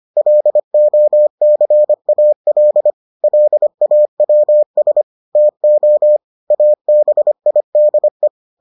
Morse code, heard as a keyed tone, about 25 wpm, 600 Hz. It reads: LOCAL LAWS TO ABIDE